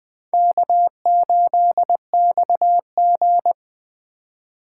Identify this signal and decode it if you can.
Morse code, heard as a keyed tone, about 20 wpm, 700 Hz. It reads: K8XG